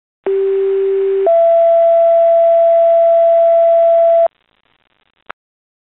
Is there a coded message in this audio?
Yes: two-tone page from 0.3 to 4.3 s, heard as a one-second tone, then a three-second tone, about 390 then 670 Hz.